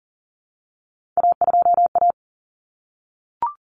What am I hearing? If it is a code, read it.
Morse code, heard as a keyed tone, about 40 wpm, 700 Hz. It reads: A2A
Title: Common Cut Numbers - 3-Digit 40wpm